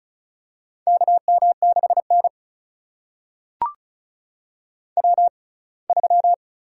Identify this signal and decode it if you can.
Morse code, heard as a keyed tone, about 36 wpm, spaced as for 24 wpm, 700 Hz. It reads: KM6N W3
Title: USA Call Signs - 4 Characters - 6x Character Spacing - 35wpm